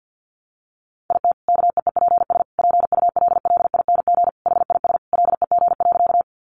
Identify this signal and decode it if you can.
Morse code, heard as a keyed tone, about 50 wpm, 700 Hz. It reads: IT KEEPS PULLING HIS LEG.
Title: Additional Easy Phrases 50wpm